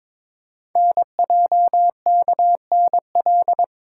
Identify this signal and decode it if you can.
Morse code, heard as a keyed tone, about 22 wpm, 700 Hz. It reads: NJKNL
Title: Code Groups - 5 Letter:Numbers - 22wpm